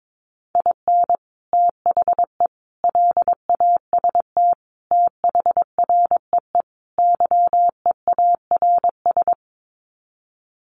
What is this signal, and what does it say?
Morse code, heard as a keyed tone, about 22 wpm, 700 Hz. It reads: IN THE LAST THREE YEARS
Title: Sets of 5 Words Encore2 22wpm